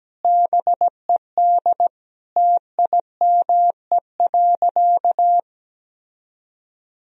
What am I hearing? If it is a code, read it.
Morse code, heard as a keyed tone, about 17 wpm, 700 Hz. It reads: BED TIME.
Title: Sentences from Top 1000 Words - Review Part 1 - 17wpm